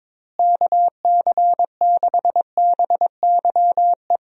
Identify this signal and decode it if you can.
Morse code, heard as a keyed tone, about 22 wpm, 700 Hz. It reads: KC6BYE
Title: USA Call Signs - 6 Characters - Repeated 3x - 22wpm